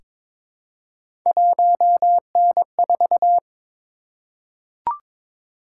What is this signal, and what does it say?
Morse code, heard as a keyed tone, about 22 wpm, 700 Hz. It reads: 1N4
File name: practice / 22wpm / Three Letter-Number Combinations - 22wpm